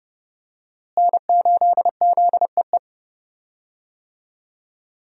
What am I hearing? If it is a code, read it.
Morse code, heard as a keyed tone, about 32 wpm, 700 Hz. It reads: N8ZEE
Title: USA Call Signs - 5 Characters - Repeated 3x - 30wpm